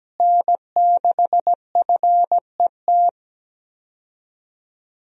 Morse code, heard as a keyed tone, about 17 wpm, 700 Hz. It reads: N6FET
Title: USA Call Signs - 5 Characters - Repeated 2x - 17wpm